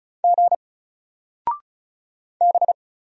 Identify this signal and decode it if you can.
Morse code, heard as a keyed tone, about 36 wpm, 700 Hz. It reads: G B
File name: Single-Letter-Number-35wpm